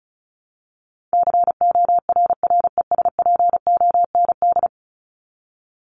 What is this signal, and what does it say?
Morse code, heard as a keyed tone, about 34 wpm, 700 Hz. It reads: CORRESPOND
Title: ING Suffix - Top 100 Words with Base Words 35wpm